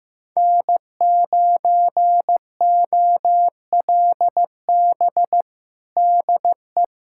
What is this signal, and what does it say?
Morse code, heard as a keyed tone, about 15 wpm, 700 Hz. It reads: N9OLB DE
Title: USA Callsign DE Callsign 15wpm